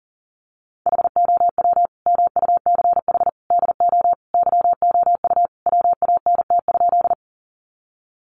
Morse code, heard as a keyed tone, about 40 wpm, 700 Hz. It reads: HOW MUCH DO YOU WANT?
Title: Sentences from Top 200 Words 40wpm